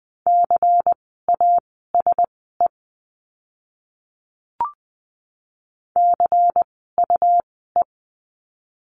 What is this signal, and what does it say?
Morse code, heard as a keyed tone, about 20 wpm, spaced as for 15 wpm, 700 Hz. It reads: CASE CUE
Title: 101 Lesson 17f - Letter u - Words 20wpm